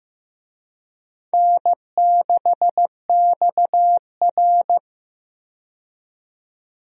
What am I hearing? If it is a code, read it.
Morse code, heard as a keyed tone, about 15 wpm, 700 Hz. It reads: N6XR
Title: USA Call Signs - 4 Characters - 15wpm